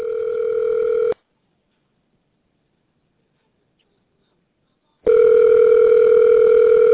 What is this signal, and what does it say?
Ringback tone (North America), 440 + 480 Hz.